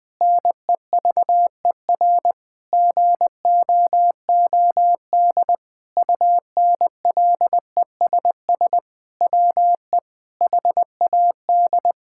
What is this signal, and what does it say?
Morse code, heard as a keyed tone, about 20 wpm, 700 Hz. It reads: NEVER GOOD UNLESS WE HAD
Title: Sentences from Top 1000 Words - Review Part 6 - 20wpm